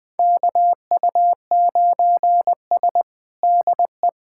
Morse code, heard as a keyed tone, about 20 wpm, 700 Hz. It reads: KU9S DE
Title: USA Callsign DE Callsign - 4 Characters - 20wpm